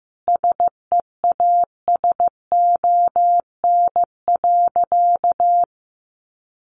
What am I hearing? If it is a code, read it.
Morse code, heard as a keyed tone, about 15 wpm, 700 Hz. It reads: SEASON.
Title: Sentences from Top 1400 Words 15wpm